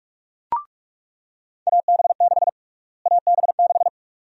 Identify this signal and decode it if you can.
Morse code, heard as a keyed tone, about 43 wpm, 700 Hz. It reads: AB6 AB6
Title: USA Call Sign Prefixes - Repeated 2x - 45wpm